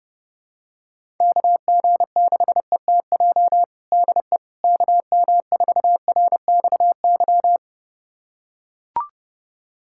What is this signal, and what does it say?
Morse code, heard as a keyed tone, about 30 wpm, 700 Hz. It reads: KG6ETJ DE KM4RXY
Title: USA Callsign DE Callsign 30wpm